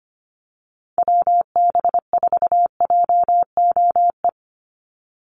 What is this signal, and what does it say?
Morse code, heard as a keyed tone, about 25 wpm, 690 Hz. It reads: WB4JOE
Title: USA Call Signs 25wpm